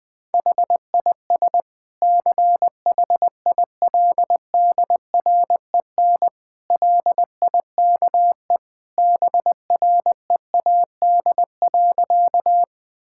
Morse code, heard as a keyed tone, about 21 wpm, 700 Hz. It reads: HIS CHILDREN LIKE BREAD.
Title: Sentences from Top 1600 Words 20wpm